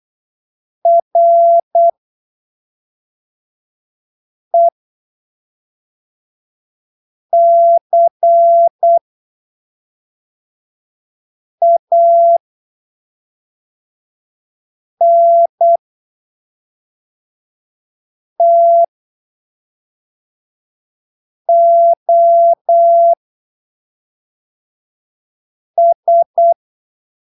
Morse code, heard as a keyed tone, about 8 wpm, spaced as for 3 wpm, 680 Hz. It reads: RECANTOS
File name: arquivos / texto / texto 29 08ppm